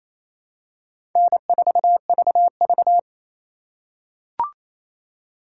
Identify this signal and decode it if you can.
Morse code, heard as a keyed tone, about 28 wpm, 700 Hz. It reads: N4VV
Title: USA Call Signs - 4x Character Spacing - 28wpm